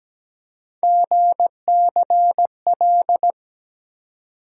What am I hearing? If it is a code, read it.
Morse code, heard as a keyed tone, about 17 wpm, 700 Hz. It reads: GCL
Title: USA Call Sign Postfixes - 3 Letters 17wpm